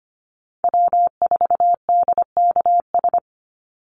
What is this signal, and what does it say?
Morse code, heard as a keyed tone, about 25 wpm, 700 Hz. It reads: W4DKS
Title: Call Signs Repeated 3x - 25wpm